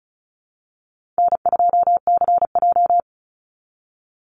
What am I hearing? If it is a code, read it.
Morse code, heard as a keyed tone, about 34 wpm, 700 Hz. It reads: N2CJ
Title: Call Signs Repeated 3x - 35wpm